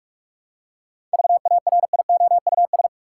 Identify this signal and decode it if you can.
Morse code, heard as a keyed tone, about 46 wpm, 700 Hz. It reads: VARIOUS
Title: Seven Letter Words - 45wpm